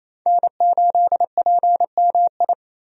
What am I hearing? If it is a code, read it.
Morse code, heard as a keyed tone, about 27 wpm, 700 Hz. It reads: N8PMI
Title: USA Call Signs - 5 Characters - Repeated 3x - 28wpm